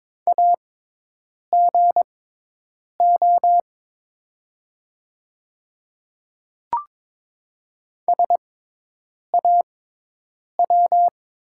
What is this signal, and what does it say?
Morse code, heard as a keyed tone, about 22 wpm, spaced as for 8 wpm, 700 Hz. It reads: AGO SAW
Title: 6x Character Spacing - Top 500 Words - 22wpm